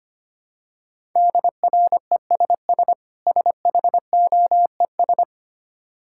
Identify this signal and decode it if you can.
Morse code, heard as a keyed tone, about 25 wpm, 700 Hz. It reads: DRESS SHOES
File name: practice / 25wpm / Difficult Words with Hints - 25wpm